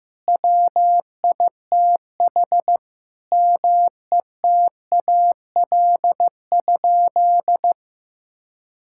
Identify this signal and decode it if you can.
Morse code, heard as a keyed tone, about 15 wpm, 700 Hz. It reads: WITH METAL?
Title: Sentences from Top 1600 Words 15wpm